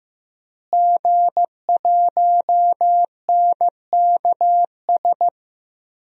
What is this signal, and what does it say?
Morse code, heard as a keyed tone, about 15 wpm, 700 Hz. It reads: G1NKS
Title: DXCC-100 Call Signs - 5 Characters - Repeated 3x - 15wpm